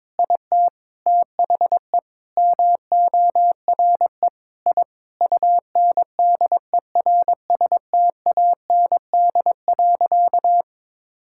Morse code, heard as a keyed tone, about 22 wpm, 700 Hz. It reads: IT THE MORE I UNDERSTAND.
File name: practice / 22wpm / Sentences with 8-Word Phrases - 22wpm